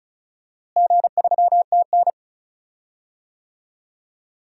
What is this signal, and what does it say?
Morse code, heard as a keyed tone, about 36 wpm, 700 Hz. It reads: G3TN
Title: DXCC-100 Call Signs - 4 Characters - Repeated 3x - 35wpm